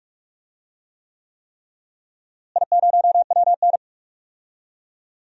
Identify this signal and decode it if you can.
Morse code, heard as a keyed tone, about 45 wpm, 700 Hz. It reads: I0WN
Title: DXCC-100 Call Signs - 4 Characters - Repeated 3x - 45wpm